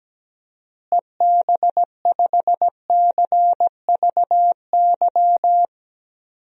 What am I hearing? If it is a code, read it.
Morse code, heard as a keyed tone, about 17 wpm, 700 Hz. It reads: EB5CVY
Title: DXCC-100 Call Signs - 6 Characters - Repeated 2x - 17wpm